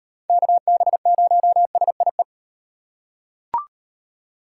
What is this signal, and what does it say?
Morse code, heard as a keyed tone, about 38 wpm, 700 Hz. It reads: KB0SIE